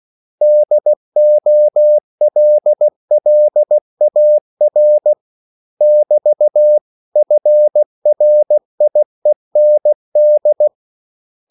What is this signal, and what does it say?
Morse code, heard as a keyed tone, about 16 wpm, 600 Hz. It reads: DOLLAR <BT> FRIEND
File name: CW / Practice / 500MostCommonEnglishWords16